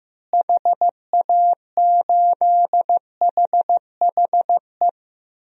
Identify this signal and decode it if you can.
Morse code, heard as a keyed tone, about 15 wpm, 700 Hz. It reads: HA8HHE
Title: DXCC-100 Call Signs - 6 Characters - Repeated 2x - 15wpm